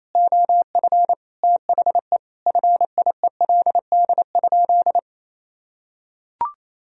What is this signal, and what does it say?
Morse code, heard as a keyed tone, about 28 wpm, 700 Hz. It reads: OF THE FIELD?
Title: Sentences from Top 1000 Words - Review Part 2 - 28wpm